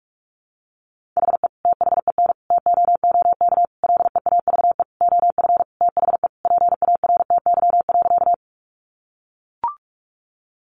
Morse code, heard as a keyed tone, about 44 wpm, 700 Hz. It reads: HE THEN TOOK LEAVE OF THE PARTY.
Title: Sentences from Top 1400 Words 45wpm